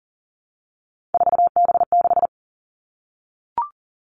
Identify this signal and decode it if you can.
Morse code, heard as a keyed tone, about 41 wpm, 690 Hz. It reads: 4B6
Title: Three Challenging Letter-Number Combinations 40wpm